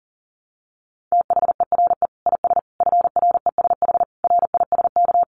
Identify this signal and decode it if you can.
Morse code, heard as a keyed tone, about 37 wpm, 700 Hz. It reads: THERE IS FRESH RISK
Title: Sentences from Top 1200 Words 40wpm